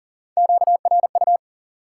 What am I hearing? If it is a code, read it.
Morse code, heard as a keyed tone, about 40 wpm, 700 Hz. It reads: QRU
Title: Top 90 QSO Elements 40wpm